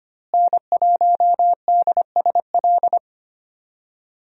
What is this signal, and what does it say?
Morse code, heard as a keyed tone, about 25 wpm, 700 Hz. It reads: N1DSL
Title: USA Call Signs - 5 Characters - Repeated 2x - 25wpm